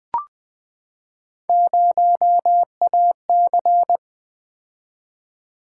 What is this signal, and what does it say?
Morse code, heard as a keyed tone, about 20 wpm, 700 Hz. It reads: 0AC
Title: Three Letter-Number Combinations - 20wpm